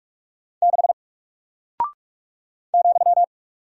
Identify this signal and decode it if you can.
Morse code, heard as a keyed tone, about 46 wpm, 700 Hz. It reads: 6 ,
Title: Single Character (Letter:Number:Punctuation:Prosign) 45wpm